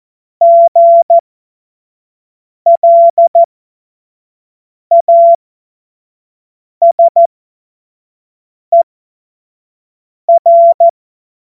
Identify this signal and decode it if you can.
Morse code, heard as a keyed tone, about 14 wpm, spaced as for 5 wpm, 680 Hz. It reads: GLASER
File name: Morsekurs / Witze / Witz 15. August 2023 25 F70